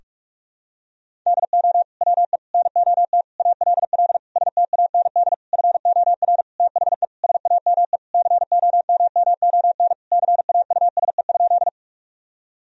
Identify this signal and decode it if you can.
Morse code, heard as a keyed tone, about 46 wpm, 700 Hz. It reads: DO WE NOT ALL STAND FOR THE SAME COMMON CAUSE?
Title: Sentences from Top 1000 Words - Review Part 3 - 45wpm